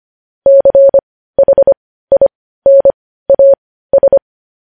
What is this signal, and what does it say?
Morse code, heard as a keyed tone, about 25 wpm, spaced as for 15 wpm, 550 Hz. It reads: CHINAS
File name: podcast / News headlines-1x-25x15